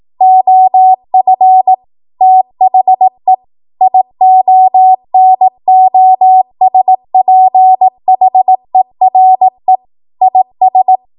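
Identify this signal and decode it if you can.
Morse code, heard as a keyed tone, about 18 wpm, 750 Hz. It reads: OF THE IONOSPHERE IS
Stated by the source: Code practice at 18 WPM, a transition run bridging 15 to 20 WPM.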